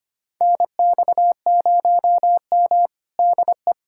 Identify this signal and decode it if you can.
Morse code, heard as a keyed tone, about 25 wpm, 700 Hz. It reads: NX0M DE
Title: USA Callsign DE Callsign - 4 Characters - 25wpm